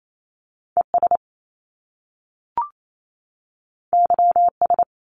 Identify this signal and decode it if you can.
Morse code, heard as a keyed tone, about 28 wpm, 700 Hz. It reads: ES YS
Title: Top Two Letter Combinations in Words 28wpm